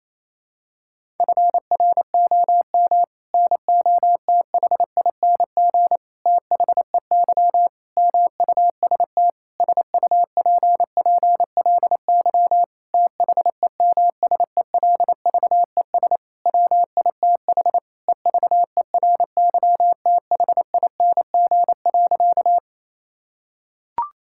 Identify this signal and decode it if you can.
Morse code, heard as a keyed tone, about 28 wpm, 700 Hz. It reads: FROM NOTHING THEY MUST SUPPLY THEMSELVES WITH EVERYTHING.
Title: Sentences from Top 1000 Words - Review Part 5 - 28wpm